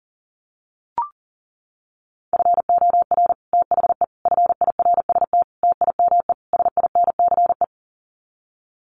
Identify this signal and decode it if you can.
Morse code, heard as a keyed tone, about 40 wpm, 700 Hz. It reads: FOR THE FIRST TIME SINCE
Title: Sets of 5 Words 40wpm